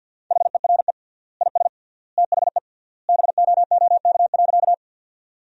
Morse code, heard as a keyed tone, about 52 wpm, 700 Hz. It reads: HERE IS THE BOOK.
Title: 3x Word-Spacing - Sentences from Top 300 Words - 50wpm